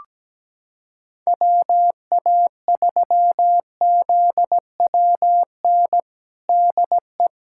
Morse code, heard as a keyed tone, about 17 wpm, 700 Hz. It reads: WA3ZWN DE
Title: USA Callsign DE Callsign 17wpm